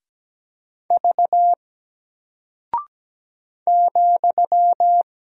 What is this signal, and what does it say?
Morse code, heard as a keyed tone, about 17 wpm, 700 Hz. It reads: V ,
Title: Single Character (Letter:Number:Punctuation:Prosign) 17wpm